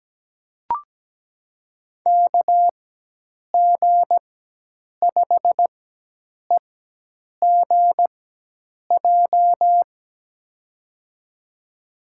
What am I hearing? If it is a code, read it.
Morse code, heard as a keyed tone, about 17 wpm, spaced as for 8 wpm, 700 Hz. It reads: KG5EGJ
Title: USA Call Signs - 6 Characters - 4x Character Spacing - 17wpm